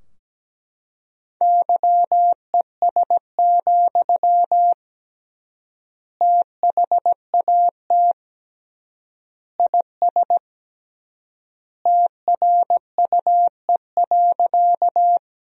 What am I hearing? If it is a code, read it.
Morse code, heard as a keyed tone, about 17 wpm, 700 Hz. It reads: YES, THAT IS TRUE.
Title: 3x Word-Spacing - Sentences from Top 500 Words - 17wpm